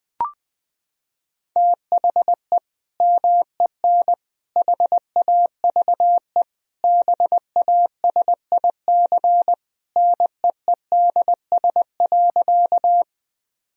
Morse code, heard as a keyed tone, about 21 wpm, 700 Hz. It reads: THE MEN HAVE BASIC NEEDS.
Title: Sentences from Top 1000 Words - Review Part 3 - 20wpm